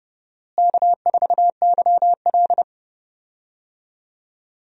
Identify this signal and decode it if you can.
Morse code, heard as a keyed tone, about 30 wpm, 700 Hz. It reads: K4YL
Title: USA Call Signs - 4 Characters - Repeated 2x - 30wpm